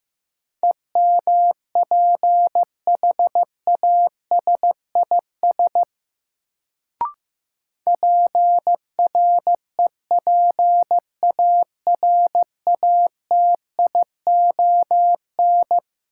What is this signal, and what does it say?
Morse code, heard as a keyed tone, about 15 wpm, 700 Hz. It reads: EMPHASIS PREPARATION
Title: Sentences from Top 1400 Words - New Word Drill Alt - 15wpm